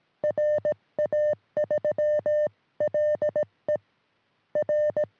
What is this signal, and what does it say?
Morse code, heard as a keyed tone, about 18 wpm, 600 Hz. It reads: RA3LE R
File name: sm7lcb / 1943 ra3le 23cm